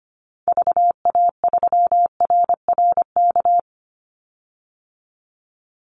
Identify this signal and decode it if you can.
Morse code, heard as a keyed tone, about 25 wpm, 700 Hz. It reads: VA3RRK
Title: Call Signs Repeated 3x - 25wpm